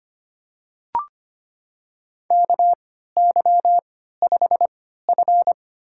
Morse code, heard as a keyed tone, about 25 wpm, spaced as for 14 wpm, 700 Hz. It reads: KY5F
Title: USA Call Signs - 4 Characters - 3x Character Spacing - 25wpm